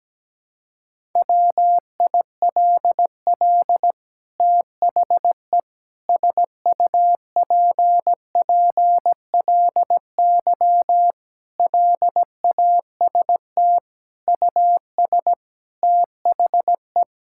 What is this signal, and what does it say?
Morse code, heard as a keyed tone, about 17 wpm, 700 Hz. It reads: WILL THE SUPPLY LAST US THE
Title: Sentences from Top 1000 Words - Review Part 5 - 17wpm